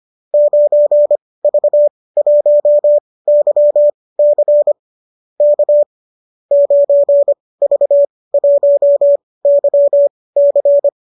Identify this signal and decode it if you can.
Morse code, heard as a keyed tone, about 25 wpm, spaced as for 18 wpm, 590 Hz. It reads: 9V1YC K 9V1YC
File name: qso207 18